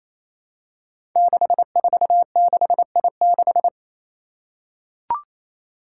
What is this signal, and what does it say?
Morse code, heard as a keyed tone, about 29 wpm, 700 Hz. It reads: 646I6
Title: Five Challenging Letter-Number Combinations 28wpm